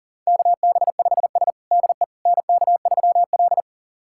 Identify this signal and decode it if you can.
Morse code, heard as a keyed tone, about 40 wpm, 700 Hz. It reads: KB5S DE NK3L
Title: USA Callsign DE Callsign - 4 Characters - 40wpm